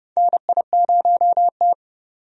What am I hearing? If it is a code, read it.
Morse code, heard as a keyed tone, about 31 wpm, 700 Hz. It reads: NI0T
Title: USA Call Signs - 4 Characters - 30wpm